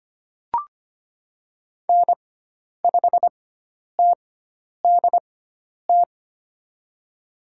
Morse code, heard as a keyed tone, about 25 wpm, spaced as for 10 wpm, 700 Hz. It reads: N5TDT
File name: Lesson 16g - Letter c - Call Signs 25wpm